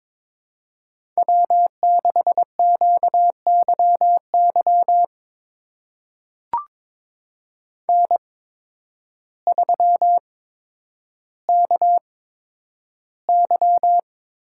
Morse code, heard as a keyed tone, about 22 wpm, spaced as for 10 wpm, 700 Hz. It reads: W6QYY N3KY
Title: USA Call Signs - 5 Characters - 8x Character Spacing - 22wpm